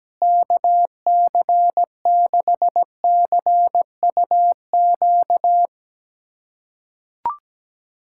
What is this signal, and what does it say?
Morse code, heard as a keyed tone, about 17 wpm, 700 Hz. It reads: KC6CUQ
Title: USA Callsign DE Callsign 17wpm